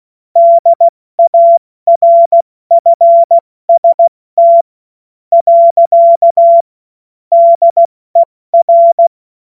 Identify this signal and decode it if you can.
Morse code, heard as a keyed tone, about 16 wpm, spaced as for 13 wpm, 680 Hz. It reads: DARFST. DER
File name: Morsekurs / Witze / Witz 14. Februar 2023 70 F80